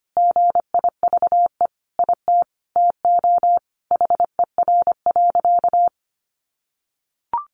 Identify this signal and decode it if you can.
Morse code, heard as a keyed tone, about 25 wpm, 700 Hz. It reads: GIVE IT TO HER.
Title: Sentences from Top 1400 Words 25wpm